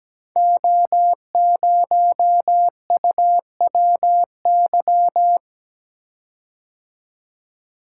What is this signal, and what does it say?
Morse code, heard as a keyed tone, about 17 wpm, 700 Hz. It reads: O0UWY
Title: Code Groups - 5 Letter:Numbers - 17wpm